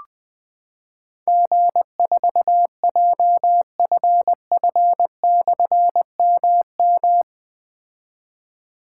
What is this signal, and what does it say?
Morse code, heard as a keyed tone, about 21 wpm, 700 Hz. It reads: G4JFF/MM
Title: Difficult Call Signs 20wpm